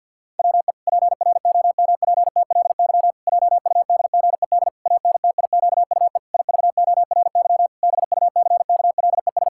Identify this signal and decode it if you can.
Morse code, heard as a keyed tone, about 51 wpm, 700 Hz. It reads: WE PROMPTLY JUDGED ANTIQUE IVORY BUCKLES
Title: Pangram Challenge v2 - 50wpm